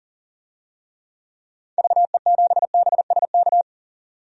Morse code, heard as a keyed tone, about 40 wpm, 700 Hz. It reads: VE7BSK
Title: DXCC-100 Call Signs - 6 Characters - 40wpm